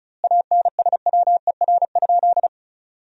Morse code, heard as a keyed tone, about 36 wpm, 700 Hz. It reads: ANSWER?
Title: 2x Word-Spacing - Sentences from Top 300 Words - 35wpm